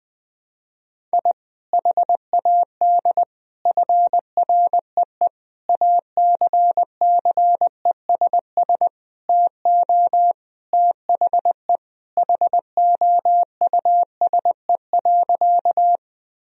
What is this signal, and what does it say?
Morse code, heard as a keyed tone, about 20 wpm, 700 Hz. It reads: I HAD FREE ACCESS TO THE HOUSE.
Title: Sentences from Top 1000 Words - Review Part 4 - 20wpm